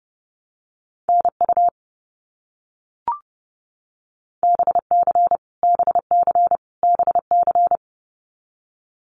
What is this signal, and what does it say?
Morse code, heard as a keyed tone, about 30 wpm, 700 Hz. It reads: NU BC BC BC
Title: Canadian Provinces and Territories - Repeated 3x - 30wpm